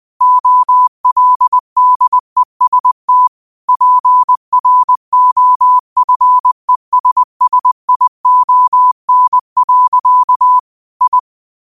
Morse code, heard as a keyed tone, about 20 wpm, 1000 Hz. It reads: OLDEST PROFESSION. I